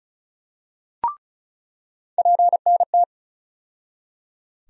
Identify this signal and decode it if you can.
Morse code, heard as a keyed tone, about 36 wpm, 700 Hz. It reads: PNT
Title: USA Call Sign Postfixes - 3 Letters 35wpm